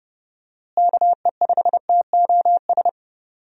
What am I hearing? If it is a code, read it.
Morse code, heard as a keyed tone, about 30 wpm, 700 Hz. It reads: KE5TOS
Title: USA Call Signs - 6 Characters - Repeated 2x - 30wpm